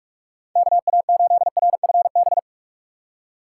Morse code, heard as a keyed tone, about 54 wpm, 700 Hz. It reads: KA8RFB
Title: Speed Racing - USA 2x3 Call Signs 45wpm